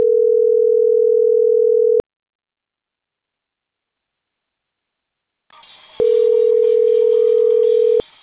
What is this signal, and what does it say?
Ringback tone (North America), 440 + 480 Hz.